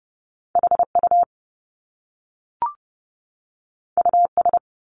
Morse code, heard as a keyed tone, about 30 wpm, 700 Hz. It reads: HU US
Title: Top Two Letter Combinations in Words 30wpm